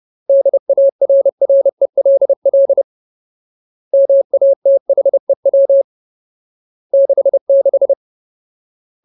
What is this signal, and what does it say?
Morse code, heard as a keyed tone, about 30 wpm, 550 Hz. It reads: DARRELL MATHEW 66
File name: Ultimate Word List / 30WPM Ultimate